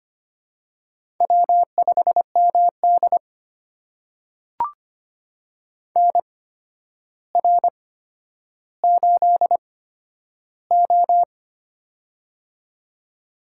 Morse code, heard as a keyed tone, about 25 wpm, spaced as for 11 wpm, 700 Hz. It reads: W5MD NR8O
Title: USA Call Signs - 4 Characters - 8x Character Spacing - 25wpm